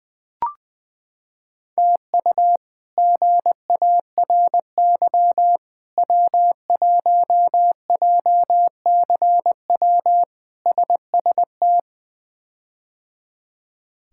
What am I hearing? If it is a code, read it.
Morse code, heard as a keyed tone, about 21 wpm, 700 Hz. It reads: TU GARY W1JCW SST
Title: K1USN SST QSO Practice - 20wpm